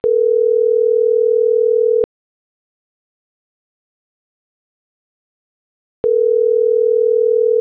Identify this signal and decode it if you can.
Ringback tone (North America), 440 + 480 Hz.